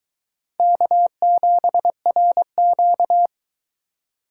Morse code, heard as a keyed tone, about 22 wpm, 700 Hz. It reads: K7RQ